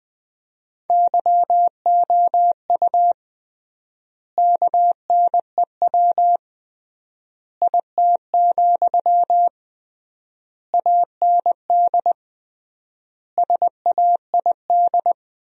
Morse code, heard as a keyed tone, about 20 wpm, 700 Hz. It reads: YOU KNEW IT, AND SAID
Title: 3x Word-Spacing - Sentences from Top 500 Words - 20wpm